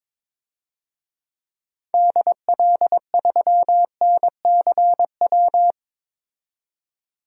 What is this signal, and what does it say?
Morse code, heard as a keyed tone, about 21 wpm, 700 Hz. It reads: DL3NCW
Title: DXCC-100 Call Signs - 6 Characters - 22wpm